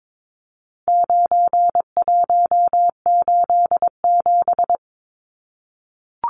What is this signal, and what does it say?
Morse code, heard as a keyed tone, about 22 wpm, 700 Hz. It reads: 9187
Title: Numbers - 4-Digit 22wpm